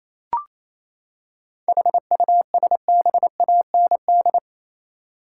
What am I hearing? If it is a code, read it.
Morse code, heard as a keyed tone, about 28 wpm, 700 Hz. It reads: HUSBAND
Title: Seven Letter Words - 28wpm